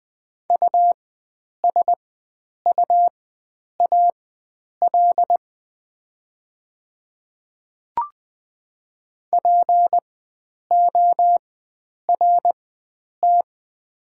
Morse code, heard as a keyed tone, about 20 wpm, spaced as for 9 wpm, 700 Hz. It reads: USUAL PORT
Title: 4x Character Spacing - Top 500 Words - 20wpm